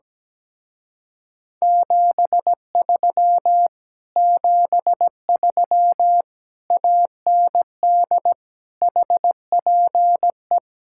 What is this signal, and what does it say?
Morse code, heard as a keyed tone, about 17 wpm, 700 Hz. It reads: 73 73 AND HPE
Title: Speed-Racing Common QSO Phrases Encore - 17wpm